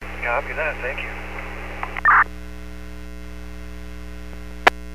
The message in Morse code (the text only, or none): none